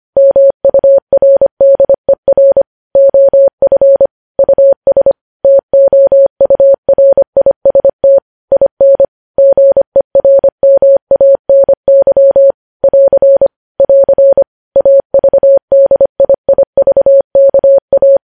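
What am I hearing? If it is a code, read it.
Morse code, heard as a keyed tone, about 25 wpm, 550 Hz. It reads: MURDER OF US TOURIST IN GERMANY <AR> <AR> AVDIIVKA
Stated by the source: Morse practice of news headlines at 25wpm.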